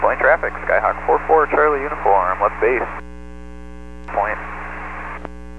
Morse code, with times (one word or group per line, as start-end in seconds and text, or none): none